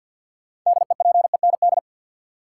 Morse code, heard as a keyed tone, about 53 wpm, 700 Hz. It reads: DEPEND
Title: Sentences from Top 1300 Words - New Word Drill - 50wpm